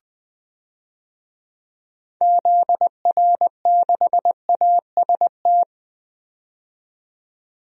Morse code, heard as a keyed tone, about 21 wpm, 700 Hz. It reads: ZR6AST
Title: DXCC-100 Call Signs - 6 Characters - 20wpm